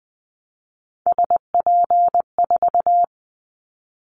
Morse code, heard as a keyed tone, about 20 wpm, 700 Hz. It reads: SP4
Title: Callsign Prefixes for Top 25 DXCC - Repeated 3x - 20wpm